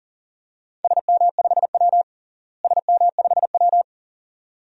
Morse code, heard as a keyed tone, about 40 wpm, 700 Hz. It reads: SM5W SM5W
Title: DXCC-100 Call Signs - 4 Characters - Repeated 2x - 40wpm